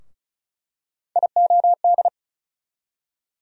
Morse code, heard as a keyed tone, about 36 wpm, 700 Hz. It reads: IOD
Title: USA Call Sign Postfixes - 3 Letters 35wpm